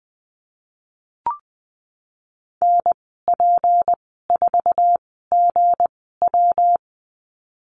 Morse code, heard as a keyed tone, about 19 wpm, spaced as for 14 wpm, 700 Hz. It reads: NP4GW
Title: 173 Lesson 28g - Number 9 - Call Signs 20wpm